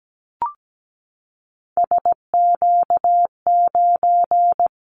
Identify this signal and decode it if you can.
Morse code, heard as a keyed tone, about 17 wpm, 700 Hz. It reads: SQ9
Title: Callsign Prefixes for Top 25 DXCC - 17wpm